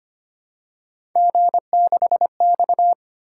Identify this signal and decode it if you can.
Morse code, heard as a keyed tone, about 25 wpm, 700 Hz. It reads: G6X
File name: Three Letter-Number Combinations - 25wpm